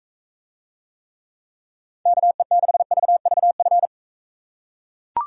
Morse code, heard as a keyed tone, about 42 wpm, 700 Hz. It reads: KE6VVF